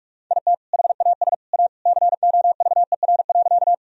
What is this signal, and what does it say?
Morse code, heard as a keyed tone, about 46 wpm, 700 Hz. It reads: IT HAS A COVER.
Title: Sentences from Top 1000 Words - Review Part 1 - 45wpm